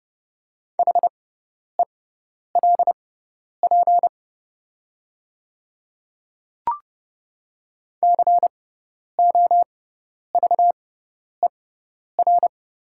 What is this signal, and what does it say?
Morse code, heard as a keyed tone, about 30 wpm, spaced as for 10 wpm, 690 Hz. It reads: HELP COVER
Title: 6x Character Spacing - Top 500 Words - 30wpm